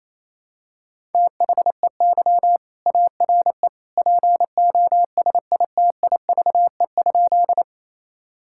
Morse code, heard as a keyed tone, about 28 wpm, 700 Hz. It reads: THEY ARE POSITIVE?
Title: Sentences from Top 1000 Words - Review Part 5 - 28wpm